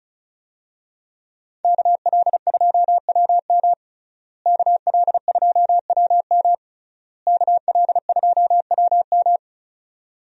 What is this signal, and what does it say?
Morse code, heard as a keyed tone, about 34 wpm, 700 Hz. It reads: KL2WM KL2WM KL2WM
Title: USA Call Signs - 5 Characters - Repeated 3x - 35wpm